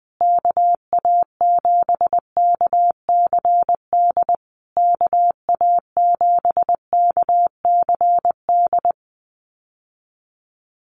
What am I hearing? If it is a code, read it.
Morse code, heard as a keyed tone, about 20 wpm, 700 Hz. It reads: KA7KCD KA7KCD
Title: USA Call Signs Repeated 3x 20wpm